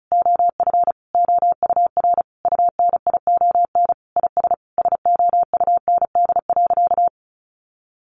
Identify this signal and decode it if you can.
Morse code, heard as a keyed tone, about 34 wpm, 700 Hz. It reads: OF OUR UNION IS SOUND.
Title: Sentences from Top 1400 Words 35wpm